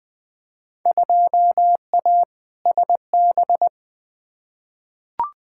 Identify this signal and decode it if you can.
Morse code, heard as a keyed tone, about 21 wpm, 700 Hz. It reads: 2A SB
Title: Field Day Exchanges - 20wpm